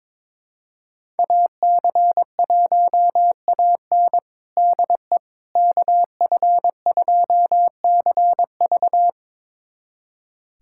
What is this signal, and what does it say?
Morse code, heard as a keyed tone, about 22 wpm, 700 Hz. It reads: AC1AN DE KF2CV
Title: USA Callsign DE Callsign - 5 Characters - 22wpm